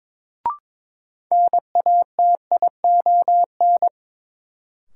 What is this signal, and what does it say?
Morse code, heard as a keyed tone, about 22 wpm, 700 Hz. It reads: NATION
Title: Sentences from Top 900 Words - New Word Drill - 22wpm